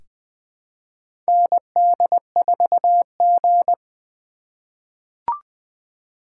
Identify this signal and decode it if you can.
Morse code, heard as a keyed tone, about 20 wpm, 700 Hz. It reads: ND4G
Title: K1USN SST QSO Practice - 20wpm